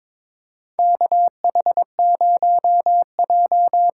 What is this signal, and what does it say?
Morse code, heard as a keyed tone, about 22 wpm, 700 Hz. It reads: KH0J
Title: USA Call Signs - 5 Characters - 8x Character Spacing - 22wpm